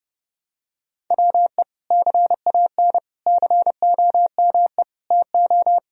Morse code, heard as a keyed tone, about 32 wpm, 700 Hz. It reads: WE CAN COME TO
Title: Sentences from Top 1000 Words - Review Part 3 - 30wpm